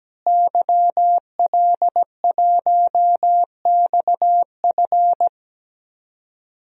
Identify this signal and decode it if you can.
Morse code, heard as a keyed tone, about 17 wpm, 700 Hz. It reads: YL1XF
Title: DXCC-100 Call Signs - 5 Characters - Repeated 3x - 17wpm